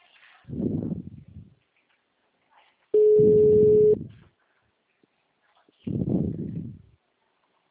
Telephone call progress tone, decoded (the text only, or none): ringback tone